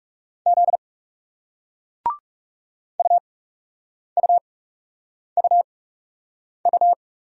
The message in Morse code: Z UUUU

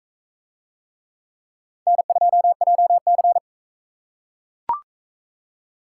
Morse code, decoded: N2JC

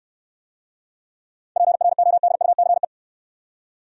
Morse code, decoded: PRODUCE

